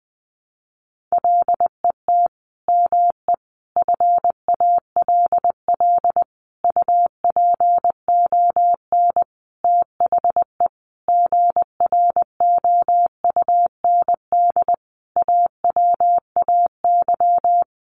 Morse code, heard as a keyed tone, about 20 wpm, 700 Hz. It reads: LET ME FALL UPON THE GROUND AWAY
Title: Sentences from Top 1100 Words 20wpm